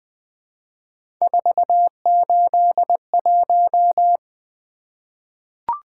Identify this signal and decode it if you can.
Morse code, heard as a keyed tone, about 20 wpm, 700 Hz. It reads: 481
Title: Three Letter-Number Combinations - 20wpm